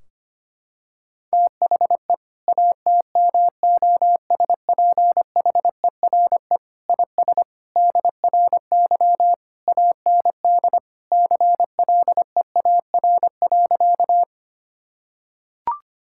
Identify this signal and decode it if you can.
Morse code, heard as a keyed tone, about 25 wpm, 700 Hz. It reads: THE ATMOSPHERE IS DRY AND CLEAR.
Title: Sentences from Top 1600 Words 25wpm